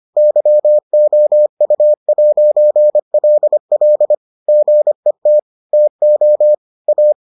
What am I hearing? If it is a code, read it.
Morse code, heard as a keyed tone, about 25 wpm, 600 Hz. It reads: YOU'LL GET TO A